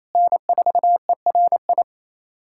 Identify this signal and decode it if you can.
Morse code, heard as a keyed tone, about 28 wpm, 700 Hz. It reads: N4ERI
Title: USA Callsign DE Callsign 28wpm